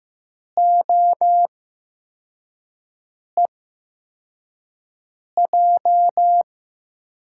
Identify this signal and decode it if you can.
Morse code, heard as a keyed tone, about 15 wpm, spaced as for 4 wpm, 700 Hz. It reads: OEJ
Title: USA Call Signs - 8x Character Spacing - 15wpm